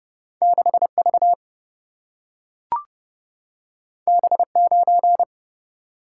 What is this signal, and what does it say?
Morse code, heard as a keyed tone, about 29 wpm, 700 Hz. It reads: 6 V B 9
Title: Single-Letter Single-Number 30wpm